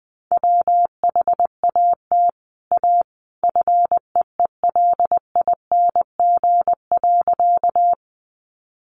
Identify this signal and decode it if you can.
Morse code, heard as a keyed tone, about 20 wpm, 700 Hz. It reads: WHAT A FEELING.
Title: Sentences from Top 1200 Words 20wpm